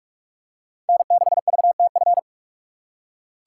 Morse code, heard as a keyed tone, about 44 wpm, 700 Hz. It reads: N6VTF